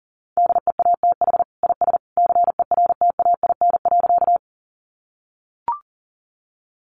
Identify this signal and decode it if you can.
Morse code, heard as a keyed tone, about 40 wpm, 700 Hz. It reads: DEATH IS CERTAIN.
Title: Sentences from Top 1100 Words 40wpm